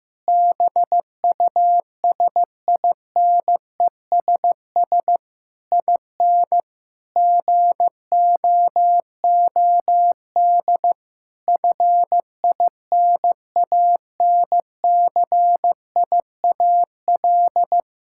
Morse code, heard as a keyed tone, about 15 wpm, 700 Hz. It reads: BUSINESS IN GOOD FINANCIAL
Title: Sentences from Top 1000 Words - Review Part 5 - 15wpm